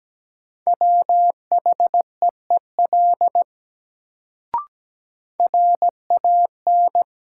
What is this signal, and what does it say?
Morse code, heard as a keyed tone, about 17 wpm, 700 Hz. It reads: WHEEL RAN
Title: Sentences from Top 500 Words - New Word Drill - 17wpm